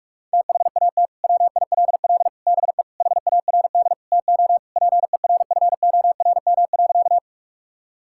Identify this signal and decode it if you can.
Morse code, heard as a keyed tone, about 46 wpm, 700 Hz. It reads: THAT WILL BE HARD TO PERFORM.
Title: Sentences from Top 1000 Words - Review Part 5 - 45wpm